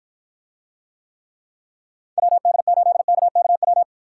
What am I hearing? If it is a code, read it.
Morse code, heard as a keyed tone, about 53 wpm, 700 Hz. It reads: WD8KKW